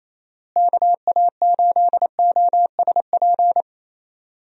Morse code, heard as a keyed tone, about 28 wpm, 700 Hz. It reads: KA8OSP
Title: USA Call Signs - 6 Characters - Repeated 2x - 28wpm